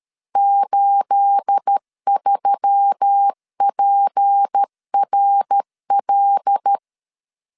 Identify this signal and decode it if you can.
Morse code, heard as a keyed tone, about 13 wpm, 800 Hz. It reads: 83PRL